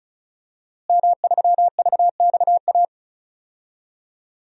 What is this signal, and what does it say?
Morse code, heard as a keyed tone, about 34 wpm, 700 Hz. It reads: M3VXA
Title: DXCC-100 Call Signs - 5 Characters - Repeated 2x - 35wpm